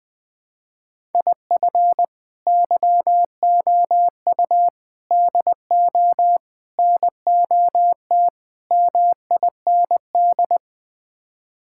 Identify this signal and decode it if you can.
Morse code, heard as a keyed tone, about 21 wpm, 700 Hz. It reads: IF YOU DO NOT MIND
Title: Build A Phrase - 5-Word Phrases - 20wpm